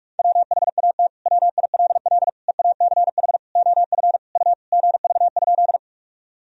Morse code, heard as a keyed tone, about 41 wpm, 700 Hz. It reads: WHAT WILL EACH OF U GV?
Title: Sentences from Top 200 Words and Abbreviations 45wpm